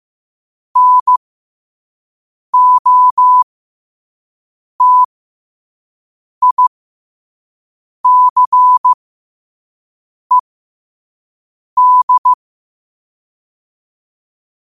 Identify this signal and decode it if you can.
Morse code, heard as a keyed tone, about 15 wpm, spaced as for 5 wpm, 1000 Hz. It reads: NOTICED
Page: Quotes for Wed, 13 Aug 2025 in Morse Code at 5 words per minute.